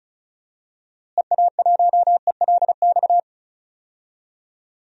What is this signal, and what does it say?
Morse code, heard as a keyed tone, about 34 wpm, 700 Hz. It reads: EA1ELX
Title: DXCC-100 Call Signs - 6 Characters - Repeated 2x - 35wpm